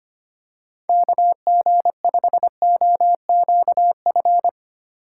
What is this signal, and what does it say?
Morse code, heard as a keyed tone, about 25 wpm, 700 Hz. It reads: KG5OQF
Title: USA Call Signs - 6 Characters - 2x Character Spacing - 25wpm